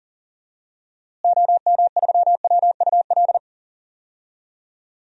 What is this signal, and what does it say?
Morse code, heard as a keyed tone, about 40 wpm, 700 Hz. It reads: OM3WUL
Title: DXCC-100 Call Signs - 6 Characters - Repeated 3x - 40wpm